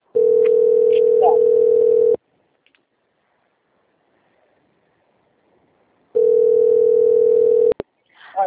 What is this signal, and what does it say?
Ringback tone (North America), 440 + 480 Hz.
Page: Limeira IV